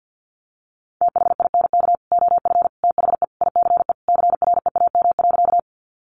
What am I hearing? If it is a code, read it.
Morse code, heard as a keyed tone, about 50 wpm, 700 Hz. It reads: THINK OF THE ICE CREAM.
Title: Sentences from Top 1500 Words 50wpm